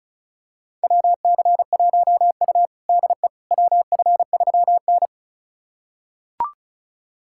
Morse code, heard as a keyed tone, about 36 wpm, 700 Hz. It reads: WC1U DE WF3N
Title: USA Callsign DE Callsign - 4 Characters - 35wpm